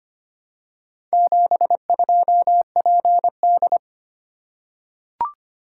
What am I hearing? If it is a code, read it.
Morse code, heard as a keyed tone, about 25 wpm, 700 Hz. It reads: 72PD
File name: practice / 25wpm / Four Letter-Number Combinations - 25wpm